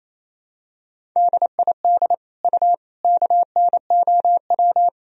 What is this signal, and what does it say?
Morse code, heard as a keyed tone, about 28 wpm, 700 Hz. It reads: DID U KNOW
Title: Sentences from Top 100 Words and Abbreviations 28wpm